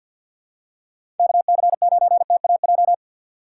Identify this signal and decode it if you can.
Morse code, heard as a keyed tone, about 53 wpm, 700 Hz. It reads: KC9TAJ
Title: USA Call Signs - 6 Characters - 3x Character Spacing - 50wpm